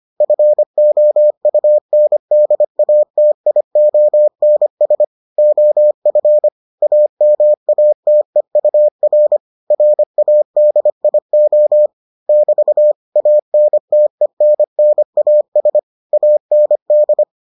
Morse code, heard as a keyed tone, about 25 wpm, 600 Hz. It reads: FOUNDATIONS OF AMATEUR RADIO <BT> ANTENNAS AND